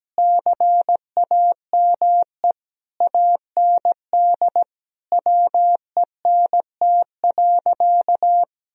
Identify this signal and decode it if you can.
Morse code, heard as a keyed tone, about 17 wpm, 700 Hz. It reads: CAME AND WENT.
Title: Sentences from Top 1000 Words - Review Part 2 - 17wpm